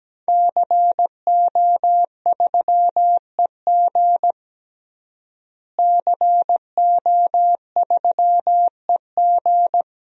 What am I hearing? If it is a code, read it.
Morse code, heard as a keyed tone, about 17 wpm, 700 Hz. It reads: CO3EG CO3EG
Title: DXCC-100 Call Signs - 5 Characters - Repeated 2x - 17wpm